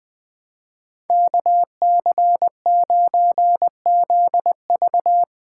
Morse code, heard as a keyed tone, about 21 wpm, 700 Hz. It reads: KC9ZV